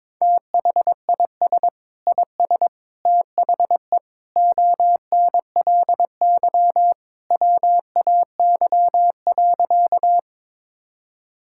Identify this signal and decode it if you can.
Morse code, heard as a keyed tone, about 22 wpm, 700 Hz. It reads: THIS IS THE ONLY WAY.
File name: practice / 22wpm / Sentences with 2-Word Phrases - 22wpm